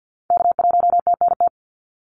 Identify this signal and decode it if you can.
Morse code, heard as a keyed tone, about 50 wpm, 700 Hz. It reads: K1TNT
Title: NA SOTA Activator Callsigns 50wpm